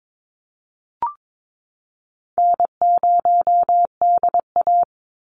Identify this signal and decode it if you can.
Morse code, heard as a keyed tone, about 22 wpm, 700 Hz. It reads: N0DA
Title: NA SOTA Chaser Callsigns 22wpm